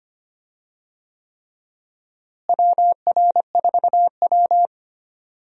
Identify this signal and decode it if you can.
Morse code, heard as a keyed tone, about 25 wpm, 700 Hz. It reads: WR4W
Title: USA Call Signs - 4 Characters - Repeated 3x - 25wpm